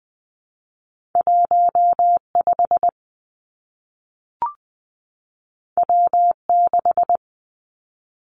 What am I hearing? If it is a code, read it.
Morse code, heard as a keyed tone, about 20 wpm, 700 Hz. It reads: 1 5 W 6
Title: Two Challenging Letter-Number Combinations 20wpm